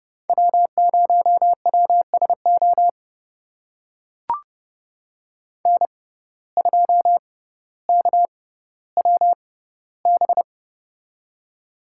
Morse code, heard as a keyed tone, about 30 wpm, spaced as for 15 wpm, 700 Hz. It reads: W0WSO N2KWB
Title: USA Call Signs - 5 Characters - 6x Character Spacing - 30wpm